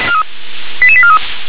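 Five-tone selective call (ZVEI), heard as quick runs of five tones, tones of about 70 ms. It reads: 88843
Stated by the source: ZVEI five-tone selective call received on an SDR.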